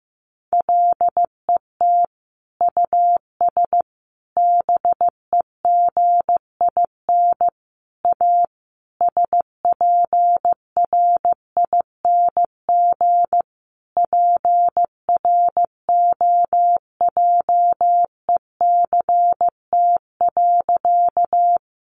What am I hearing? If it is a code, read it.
Morse code, heard as a keyed tone, about 15 wpm, 700 Hz. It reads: LET US BEGIN A SPRING PROJECT.
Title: Sentences from Top 1400 Words 15wpm